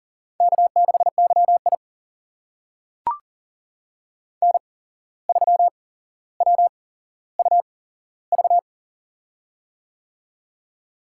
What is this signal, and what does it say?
Morse code, heard as a keyed tone, about 40 wpm, spaced as for 16 wpm, 700 Hz. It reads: K6YI N3WUV
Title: USA Call Signs - 8x Character Spacing - 40wpm